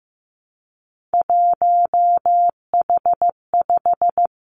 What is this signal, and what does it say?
Morse code, heard as a keyed tone, about 15 wpm, 700 Hz. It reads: 1H5
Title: Three Challenging Letter-Number Combinations 15wpm